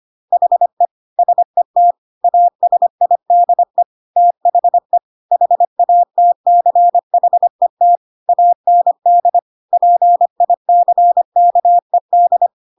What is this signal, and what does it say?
Morse code, heard as a keyed tone, about 25 wpm, 700 Hz. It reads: HE SET ASIDE THE HATCHET AND PICKED
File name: Pinocchio / Pinocchio ch1 25 25 ik1hge